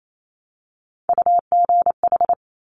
Morse code, heard as a keyed tone, about 28 wpm, 700 Hz. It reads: UGH
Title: Top Three Letter Combinations in Words 28wpm